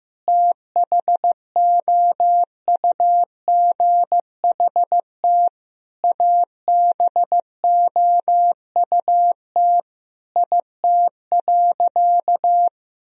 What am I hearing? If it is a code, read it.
Morse code, heard as a keyed tone, about 15 wpm, 700 Hz. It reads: THOUGHT ABOUT IT.
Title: Sentences from Top 1000 Words - Review Part 1 - 15wpm